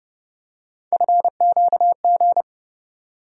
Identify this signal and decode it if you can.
Morse code, heard as a keyed tone, about 30 wpm, 700 Hz. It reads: FQG
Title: USA Call Sign Postfixes - 3 Letters 30wpm